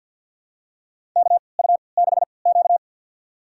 Morse code, heard as a keyed tone, about 52 wpm, spaced as for 29 wpm, 700 Hz. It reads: KU6Q
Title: USA Call Signs - 3x Character Spacing - 50wpm